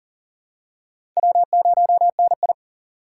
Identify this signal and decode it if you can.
Morse code, heard as a keyed tone, about 40 wpm, 700 Hz. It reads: W0NI
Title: USA Call Signs - 4 Characters - 2x Character Spacing - 40wpm